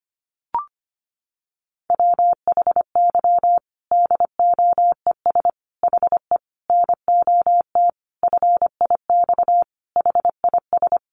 Morse code, heard as a keyed tone, about 25 wpm, 700 Hz. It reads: WHY DOES HE NOT FIX HIS
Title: Sentences from Top 1400 Words 25wpm